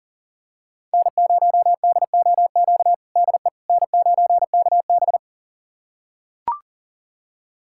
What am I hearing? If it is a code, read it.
Morse code, heard as a keyed tone, about 40 wpm, 700 Hz. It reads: N0DOQ DE N9KB
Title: USA Callsign DE Callsign 40wpm